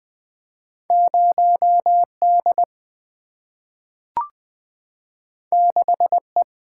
Morse code, heard as a keyed tone, about 20 wpm, 700 Hz. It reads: 0 D 6 E
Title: Two Letter-Number Combinations - 20wpm